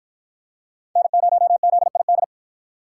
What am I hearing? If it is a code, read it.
Morse code, heard as a keyed tone, about 53 wpm, 700 Hz. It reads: N0ZID